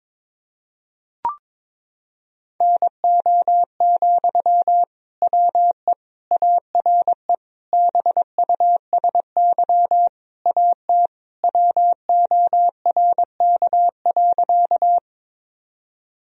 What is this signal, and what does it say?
Morse code, heard as a keyed tone, about 21 wpm, 700 Hz. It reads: NO, WE ARE BUSY AT WORK.
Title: Sentences from Top 1000 Words - Review Part 3 - 22wpm